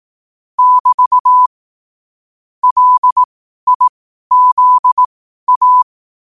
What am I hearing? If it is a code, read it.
Morse code, heard as a keyed tone, about 18 wpm, spaced as for 12 wpm, 1000 Hz. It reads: <BT> LIZA
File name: qotdfiles / 12wpm